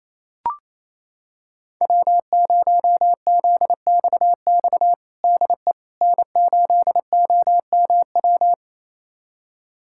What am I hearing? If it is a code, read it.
Morse code, heard as a keyed tone, about 28 wpm, 700 Hz. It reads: W0ZXX DE N8OMW
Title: USA Callsign DE Callsign - 5 Characters - 28wpm